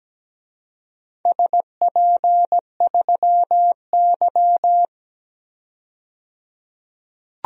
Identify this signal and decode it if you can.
Morse code, heard as a keyed tone, about 17 wpm, 700 Hz. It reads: SP3Y
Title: DXCC-100 Call Signs - 4 Characters - 17wpm